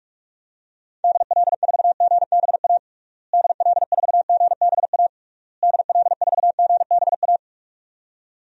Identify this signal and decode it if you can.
Morse code, heard as a keyed tone, about 46 wpm, 700 Hz. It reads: DL4GBA DL4GBA DL4GBA
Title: DXCC-100 Call Signs - 6 Characters - Repeated 3x - 45wpm